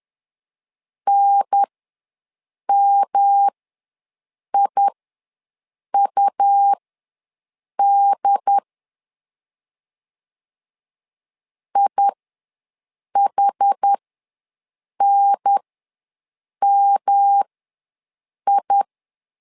Morse code, heard as a keyed tone, about 10 wpm, spaced as for 6 wpm, 800 Hz. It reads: NMIUD IHNMI
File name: LEZIONI / 3 MWRO / 3 M ws 10wpm 2121